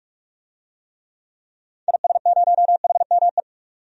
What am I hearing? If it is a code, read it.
Morse code, heard as a keyed tone, about 46 wpm, 700 Hz. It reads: IS0HME
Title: DXCC-100 Call Signs - 6 Characters - 45wpm